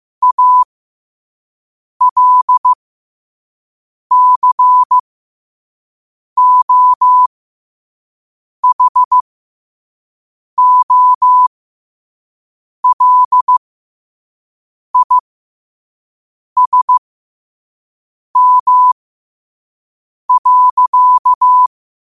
Morse code, heard as a keyed tone, about 15 wpm, spaced as for 5 wpm, 1000 Hz. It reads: ALCOHOLISM.
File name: qotdfiles / 5wpm